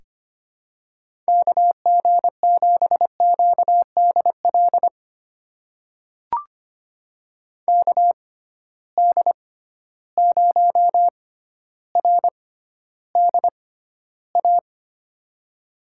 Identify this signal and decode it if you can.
Morse code, heard as a keyed tone, about 25 wpm, spaced as for 13 wpm, 700 Hz. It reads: KG7QDL KD0RDA
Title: USA Call Signs - 6 Characters - 6x Character Spacing - 25wpm